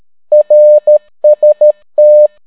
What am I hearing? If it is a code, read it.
Morse code, heard as a keyed tone, about 13 wpm, 600 Hz. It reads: RST